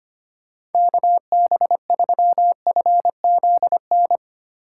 Morse code, heard as a keyed tone, about 31 wpm, 700 Hz. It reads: KB3FZN
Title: Speed Racing - USA 2x3 Call Signs 25wpm